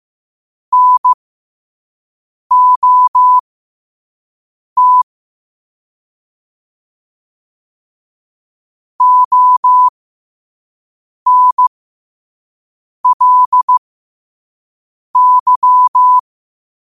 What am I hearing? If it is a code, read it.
Morse code, heard as a keyed tone, about 15 wpm, spaced as for 5 wpm, 1000 Hz. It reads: NOT ONLY
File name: qotdfiles / 5wpm